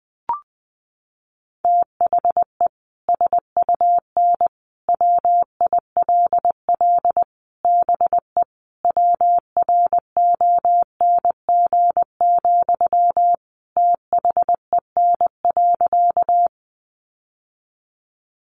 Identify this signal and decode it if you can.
Morse code, heard as a keyed tone, about 20 wpm, 700 Hz. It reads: THE SUN WILL BE WRONG, THEN.
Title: Sentences from Top 900 Words 20wpm